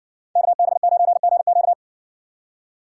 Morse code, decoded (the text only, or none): KB9GQ